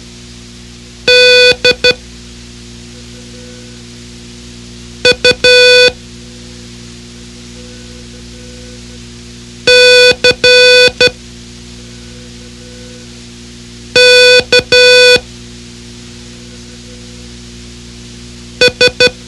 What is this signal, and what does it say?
Morse code, heard as a keyed tone, about 12 wpm, spaced as for 3 wpm, 500 Hz. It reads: DUCKS
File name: ham radio / CodeQuick / Code Quick 04